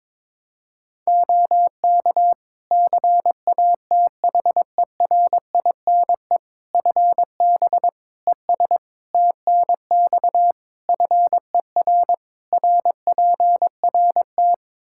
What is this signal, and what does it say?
Morse code, heard as a keyed tone, about 22 wpm, 700 Hz. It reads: OK CATHERINE FB ES TNX FER RPRT
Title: Practice QSO - Exchange 2 - Info - 22wpm